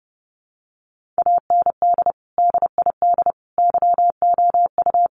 Morse code, heard as a keyed tone, about 30 wpm, 700 Hz. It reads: AND DID YOU
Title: Sentences from Top 1500 Words 30wpm